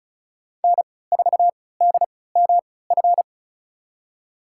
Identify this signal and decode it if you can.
Morse code, heard as a keyed tone, about 36 wpm, spaced as for 20 wpm, 700 Hz. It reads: N4DMF
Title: USA Call Signs - 5 Characters - 3x Character Spacing - 35wpm